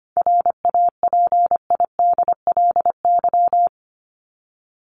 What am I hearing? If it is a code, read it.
Morse code, heard as a keyed tone, about 25 wpm, 700 Hz. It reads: RAPIDLY
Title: LY Suffix - Top 100 Words with Base Words 25wpm